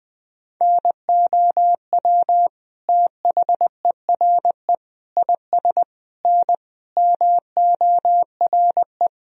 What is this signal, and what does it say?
Morse code, heard as a keyed tone, about 19 wpm, 700 Hz. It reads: NOW THERE IS N MORE
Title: Sentences from Top 100 Words and Abbreviations 20wpm